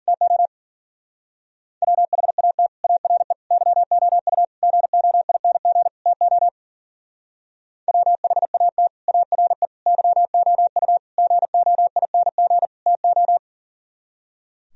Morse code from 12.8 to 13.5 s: TO

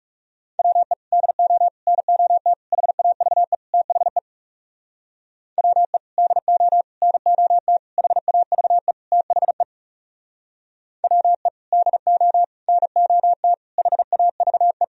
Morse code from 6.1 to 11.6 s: DO NOT HAVE THE WE

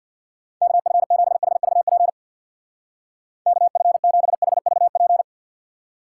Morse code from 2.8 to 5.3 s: KF7HVP

